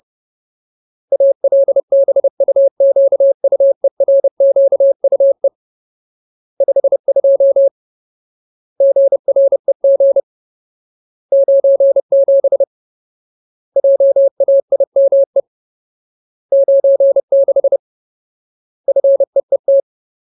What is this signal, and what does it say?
Morse code, heard as a keyed tone, about 30 wpm, 550 Hz. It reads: ALBUQUERQUE 52 GREG 97 JAIME 96 FEET
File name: Ultimate Word List / 30WPM Ultimate